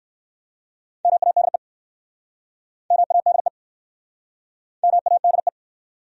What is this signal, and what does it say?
Morse code, heard as a keyed tone, about 64 wpm, 700 Hz. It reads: MADE MADE MADE